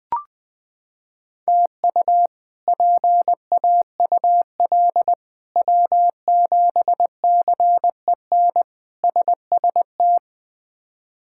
Morse code, heard as a keyed tone, about 20 wpm, 700 Hz. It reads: TU PAUL W7CEN SST
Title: K1USN SST QSO Practice - 20wpm